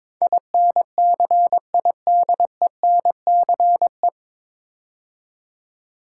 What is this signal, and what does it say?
Morse code, heard as a keyed tone, about 22 wpm, 700 Hz. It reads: INCIDENCE
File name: 9 Letter Words - First 3 Letters Given - 22wpm